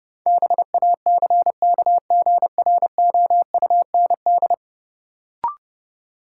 Morse code, heard as a keyed tone, about 30 wpm, 700 Hz. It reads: BACKGROUND
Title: Sentences from Top 1300 Words - New Word Drill Alt - 30wpm